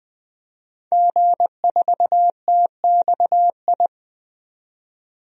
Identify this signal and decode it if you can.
Morse code, heard as a keyed tone, about 20 wpm, 700 Hz. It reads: G4TXI
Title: DXCC-100 Call Signs - 5 Characters - 20wpm